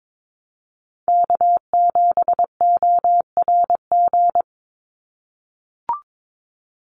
Morse code, heard as a keyed tone, about 22 wpm, 700 Hz. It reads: K7ORG
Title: NA SOTA Chaser Callsigns 22wpm